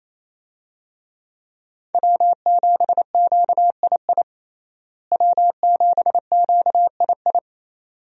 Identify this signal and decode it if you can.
Morse code, heard as a keyed tone, about 28 wpm, 700 Hz. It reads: W7QII W7QII
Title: USA Call Signs - 5 Characters - Repeated 2x - 28wpm